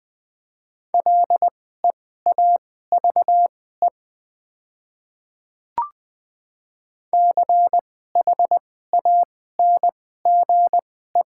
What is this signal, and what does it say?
Morse code, heard as a keyed tone, about 20 wpm, spaced as for 14 wpm, 700 Hz. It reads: LEAVE CHANGE
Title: 2x Character Spacing - Top 500 Words - 20wpm